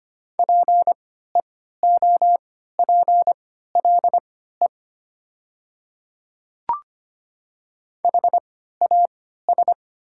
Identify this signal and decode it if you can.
Morse code, heard as a keyed tone, about 25 wpm, spaced as for 14 wpm, 700 Hz. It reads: PEOPLE HAS
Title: 3x Character Spacing - Top 100 Words - 25wpm